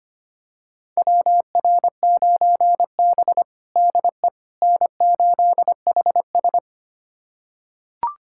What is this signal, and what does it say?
Morse code, heard as a keyed tone, about 25 wpm, 700 Hz. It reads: WR9B DE N8HS
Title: USA Callsign DE Callsign - 4 Characters - 25wpm